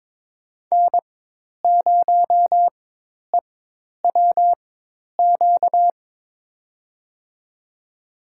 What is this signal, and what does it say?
Morse code, heard as a keyed tone, about 23 wpm, spaced as for 10 wpm, 700 Hz. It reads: N0EWQ
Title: USA Call Signs - 5 Characters - 4x Character Spacing - 22wpm